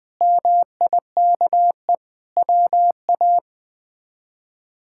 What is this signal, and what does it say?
Morse code, heard as a keyed tone, about 21 wpm, 700 Hz. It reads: MIKE WA
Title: K1USN SST QSO Practice - 20wpm